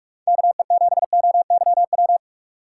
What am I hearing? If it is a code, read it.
Morse code, heard as a keyed tone, about 59 wpm, 700 Hz. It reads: KE7OYW